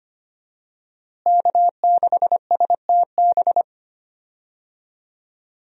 Morse code, heard as a keyed tone, about 25 wpm, 700 Hz. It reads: K6STB
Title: USA Call Signs - 5 Characters - Repeated 3x - 25wpm